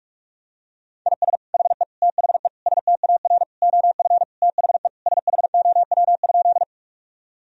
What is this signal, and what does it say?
Morse code, heard as a keyed tone, about 46 wpm, 700 Hz. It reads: IS HE THE STAR OF THE SHOW?
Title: Sentences from Top 1000 Words - Review Part 2 - 45wpm